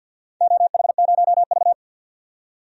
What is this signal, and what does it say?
Morse code, heard as a keyed tone, about 48 wpm, 700 Hz. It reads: OH0V
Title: DXCC-100 Call Signs - 4 Characters - 50wpm